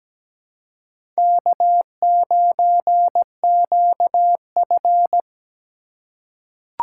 Morse code, heard as a keyed tone, about 17 wpm, 700 Hz. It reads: K9QF
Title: USA Call Signs - 2x Character Spacing - 17wpm